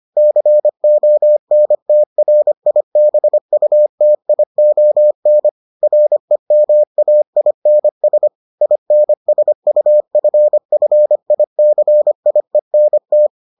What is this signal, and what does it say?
Morse code, heard as a keyed tone, about 25 wpm, 600 Hz. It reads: CONTRIBUTION REMAINS INSUFFICIENT